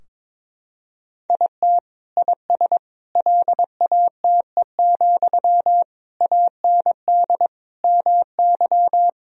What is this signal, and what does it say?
Morse code, heard as a keyed tone, about 22 wpm, 700 Hz. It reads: IT IS LATE, AND MY
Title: Sentences from Top 1000 Words - Review Part 3 - 22wpm